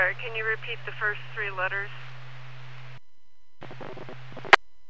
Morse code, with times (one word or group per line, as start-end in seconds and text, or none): none